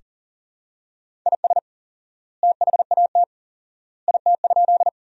Morse code, heard as a keyed tone, about 40 wpm, 700 Hz. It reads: IS THAT IT?
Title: 4x Word-Spacing - Sentences from Top 100 Words - 40wpm